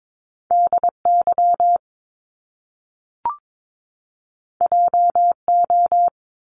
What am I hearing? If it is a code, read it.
Morse code, heard as a keyed tone, about 22 wpm, 700 Hz. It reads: DY JO
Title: Top Two Letter Combinations in Words 22wpm